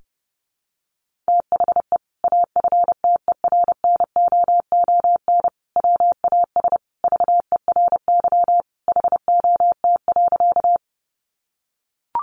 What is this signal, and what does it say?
Morse code, heard as a keyed tone, about 30 wpm, 700 Hz. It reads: THE AFTERNOON WAS VERY HOT.
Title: Sentences from Top 1500 Words 30wpm